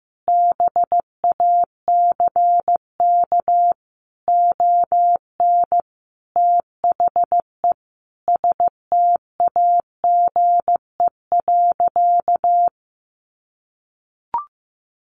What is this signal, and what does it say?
Morse code, heard as a keyed tone, about 15 wpm, 700 Hz. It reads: BACK ON THE STAGE.
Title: Sentences from Top 1200 Words 15wpm